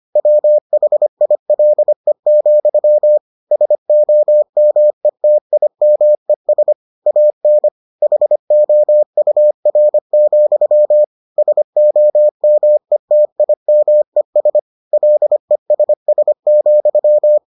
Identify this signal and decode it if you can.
Morse code, heard as a keyed tone, about 25 wpm, 600 Hz. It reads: WHILE, SOMETIMES AN HOUR, SOMETIMES LESS,